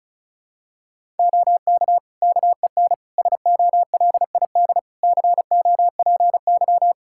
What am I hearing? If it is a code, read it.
Morse code, heard as a keyed tone, about 36 wpm, 700 Hz. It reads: OK KEN SOLID COPY
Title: Practice QSO - Exchange 3 - More Info - 35wpm